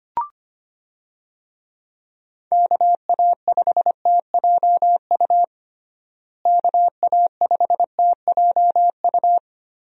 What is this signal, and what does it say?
Morse code, heard as a keyed tone, about 25 wpm, 700 Hz. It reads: KA5TJU KA5TJU
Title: USA Call Signs - 6 Characters - Repeated 2x - 25wpm